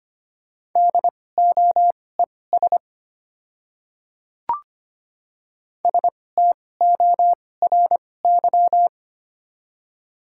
Morse code, heard as a keyed tone, about 25 wpm, spaced as for 18 wpm, 690 Hz. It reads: DOES STORY
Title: 2x Character Spacing - Top 300 Words - 25wpm